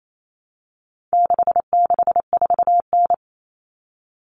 Morse code, heard as a keyed tone, about 28 wpm, 700 Hz. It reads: 664N
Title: Common Cut Numbers - 4-Digit 28wpm